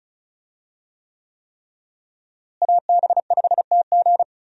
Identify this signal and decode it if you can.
Morse code, heard as a keyed tone, about 36 wpm, 700 Hz. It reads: AB5TG
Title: USA Call Signs - 5 Characters - Repeated 3x - 35wpm